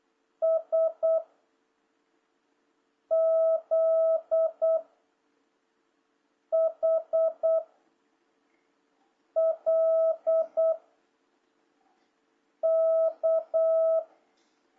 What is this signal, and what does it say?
Morse code, heard as a keyed tone, about 8 wpm, 660 Hz. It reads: S Z H L K
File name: jh8cbh / 2026cw